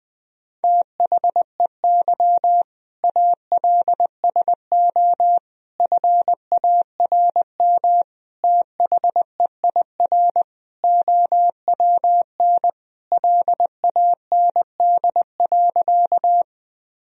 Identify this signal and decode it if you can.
Morse code, heard as a keyed tone, about 20 wpm, 700 Hz. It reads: THEY ALSO FARM THEIR OWN LAND.
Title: Sentences from Top 1000 Words - Review Part 2 - 20wpm